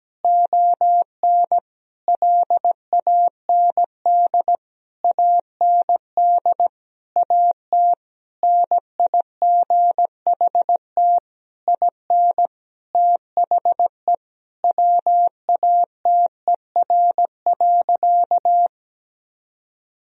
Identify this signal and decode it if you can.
Morse code, heard as a keyed tone, about 17 wpm, 700 Hz. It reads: ON LAND AND AT NIGHT IN THE WATER.
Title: Sentences from Top 1000 Words - Review Part 1 - 17wpm